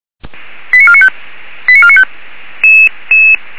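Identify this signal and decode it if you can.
Five-tone selective call (ZVEI), heard as quick runs of five tones, tones of about 70 ms. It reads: 89495 89495
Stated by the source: ZVEI five-tone selective call received via SDR, on the audio server of Feuerwehr Orlamünde (fire brigade).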